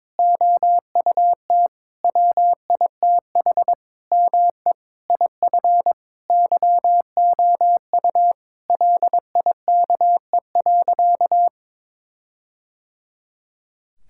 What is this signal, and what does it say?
Morse code, heard as a keyed tone, about 22 wpm, 700 Hz. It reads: OUT WITH ME IF YOU LIKE.
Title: Sentences from Top 1600 Words 22wpm